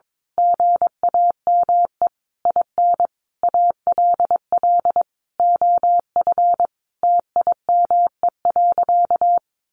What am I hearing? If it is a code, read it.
Morse code, heard as a keyed tone, about 22 wpm, 700 Hz. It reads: GAME IN ALL OF TIME.
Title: A Sound Of Thunder 22wpm